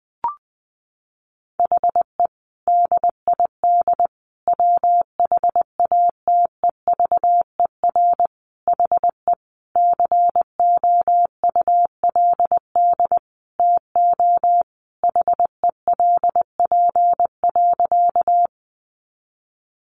Morse code, heard as a keyed tone, about 20 wpm, 700 Hz. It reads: HE DID WHATEVER HE COULD TO HELP.
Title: Sentences from Top 1100 Words 20wpm